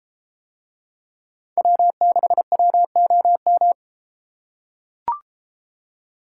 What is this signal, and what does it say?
Morse code, heard as a keyed tone, about 33 wpm, 700 Hz. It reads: W6WOM